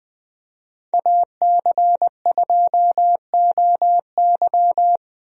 Morse code, heard as a keyed tone, about 21 wpm, 700 Hz. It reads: AC2OY
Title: USA Call Signs - 5 Characters - Repeated 2x - 20wpm